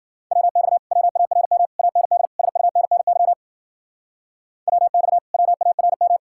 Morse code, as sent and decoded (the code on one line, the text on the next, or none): .-- -..- / .-- .- .-. -- / .- -. -.. / ... ..- -. -. -.-- / .-- -..- / .-- .- .-. --
WX WARM AND SUNNY WX WARM